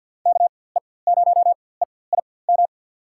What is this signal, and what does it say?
Morse code, heard as a keyed tone, about 50 wpm, spaced as for 23 wpm, 700 Hz. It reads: KE0EIM
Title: USA Call Signs - 4x Character Spacing - 50wpm